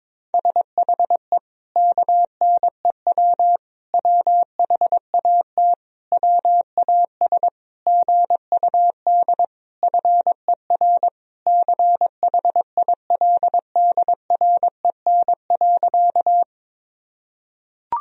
Morse code, heard as a keyed tone, about 22 wpm, 700 Hz. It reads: SHE KNEW WHAT WAS GUD FER CHILDREN.
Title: Sentences from Top 400 Words and Abbreviations 22wpm